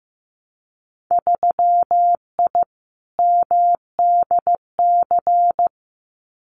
Morse code, heard as a keyed tone, about 15 wpm, 700 Hz. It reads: 3I MDC
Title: Winter Field Day Exchanges Encore 15wpm